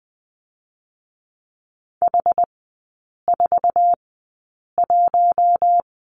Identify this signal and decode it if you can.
Morse code, heard as a keyed tone, about 20 wpm, spaced as for 8 wpm, 700 Hz. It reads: H41
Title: 060 Lesson 11e - Letter h - Three Characters 20wpm